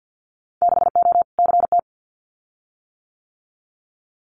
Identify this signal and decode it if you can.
Morse code, heard as a keyed tone, about 50 wpm, 700 Hz. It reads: 6O CT
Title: Winter Field Day Exchanges Encore 50wpm